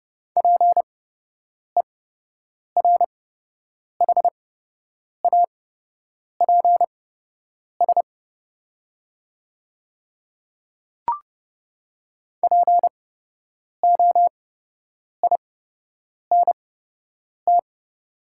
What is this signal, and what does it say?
Morse code, heard as a keyed tone, about 30 wpm, spaced as for 8 wpm, 700 Hz. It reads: PERHAPS POINT
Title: 8x Character Spacing - Top 500 Words - 30wpm